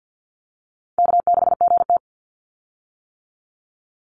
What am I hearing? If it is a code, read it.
Morse code, heard as a keyed tone, about 50 wpm, 700 Hz. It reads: K6GT
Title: Call Signs - 50wpm